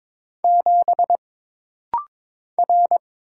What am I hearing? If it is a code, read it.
Morse code, heard as a keyed tone, about 22 wpm, 700 Hz. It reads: 7 R
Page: This rapid-fire timing is geared to push your ICR (Instant Character Recognition) skills.